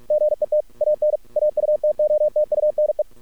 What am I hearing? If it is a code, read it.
Morse code, heard as a keyed tone, about 46 wpm, 600 Hz. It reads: GET AN AUTOTUNE
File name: archived / CRSnet